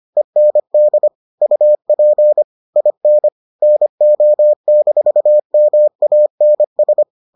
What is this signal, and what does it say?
Morse code, heard as a keyed tone, about 25 wpm, 600 Hz. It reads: END UP IN NO-MANS